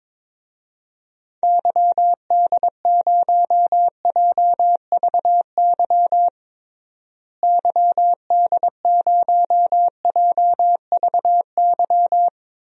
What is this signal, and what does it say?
Morse code, heard as a keyed tone, about 22 wpm, 700 Hz. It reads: YD0JVY YD0JVY
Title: DXCC-100 Call Signs - 6 Characters - Repeated 2x - 22wpm